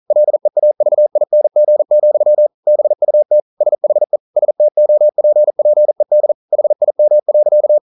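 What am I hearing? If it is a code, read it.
Morse code, heard as a keyed tone, about 44 wpm, 600 Hz. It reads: LEAVING, BUT SHE STOPPED HIM.
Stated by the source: Morse code at 40 wpm, practice text, CW med Gnister.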